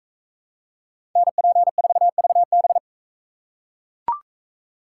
Morse code, heard as a keyed tone, about 42 wpm, 700 Hz. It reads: NP4VB